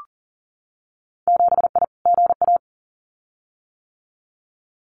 Morse code, heard as a keyed tone, about 40 wpm, 700 Hz. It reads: 7I GA
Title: Winter Field Day Exchanges Encore 40wpm